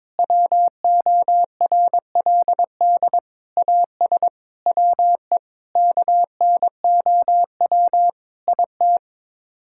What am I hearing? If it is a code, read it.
Morse code, heard as a keyed tone, about 21 wpm, 700 Hz. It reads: WORLD AS WE KNOW IT
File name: practice / 22wpm / Sets of 8 Words - 22wpm